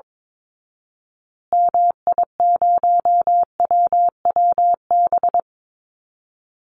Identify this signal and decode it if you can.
Morse code, heard as a keyed tone, about 22 wpm, 700 Hz. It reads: MI0WWB
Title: Call Signs Repeated 2x - 22wpm